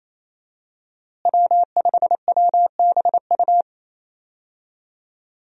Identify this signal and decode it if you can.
Morse code, heard as a keyed tone, about 27 wpm, 700 Hz. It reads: W5WBU
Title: USA Call Signs - 5 Characters - Repeated 2x - 28wpm